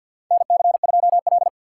Morse code, heard as a keyed tone, about 49 wpm, 700 Hz. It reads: NC2L
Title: USA Call Signs - 4 Characters - 2x Character Spacing - 50wpm